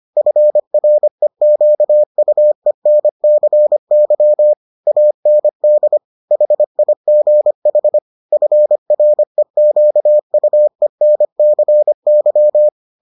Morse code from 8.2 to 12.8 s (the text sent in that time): FREQUENCY